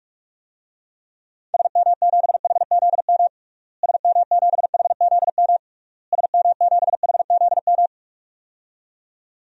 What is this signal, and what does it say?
Morse code, heard as a keyed tone, about 42 wpm, 700 Hz. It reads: SM7HZM SM7HZM SM7HZM
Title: DXCC-100 Call Signs - 6 Characters - Repeated 3x - 45wpm